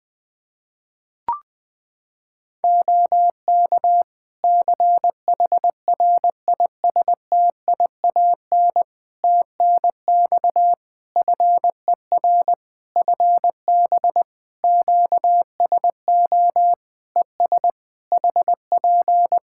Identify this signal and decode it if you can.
Morse code, heard as a keyed tone, about 20 wpm, 700 Hz. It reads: OK CHRISTIAN TNX FER FB QSO ES HP
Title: Practice QSO - Exchange 4 - Ending - 20wpm